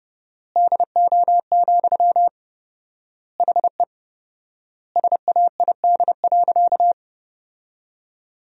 Morse code, heard as a keyed tone, about 30 wpm, 700 Hz. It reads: DO, HE SAID.
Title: 4x Word-Spacing - Sentences from Top 100 Words - 30wpm